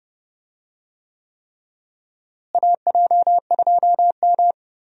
Morse code, heard as a keyed tone, about 30 wpm, 700 Hz. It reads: AJ2M
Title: USA Call Signs - 4 Characters - Repeated 3x - 30wpm